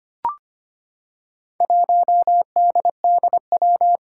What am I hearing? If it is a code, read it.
Morse code, heard as a keyed tone, about 25 wpm, 700 Hz. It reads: 1DDW
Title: Four Challenging Letter-Number Combinations 25wpm